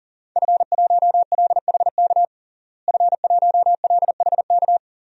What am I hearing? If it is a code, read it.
Morse code, heard as a keyed tone, about 40 wpm, 700 Hz. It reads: F1LHK F1LHK
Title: DXCC-100 Call Signs - 5 Characters - Repeated 2x - 40wpm